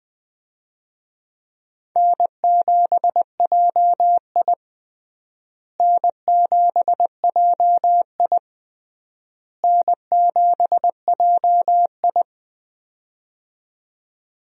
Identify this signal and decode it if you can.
Morse code, heard as a keyed tone, about 20 wpm, 700 Hz. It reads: N7JI N7JI N7JI
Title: USA Call Signs - 4 Characters - Repeated 3x - 20wpm